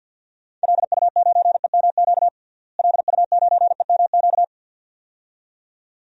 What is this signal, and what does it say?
Morse code, heard as a keyed tone, about 50 wpm, 700 Hz. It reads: LU9EMQ LU9EMQ
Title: DXCC-100 Call Signs - 6 Characters - Repeated 2x - 50wpm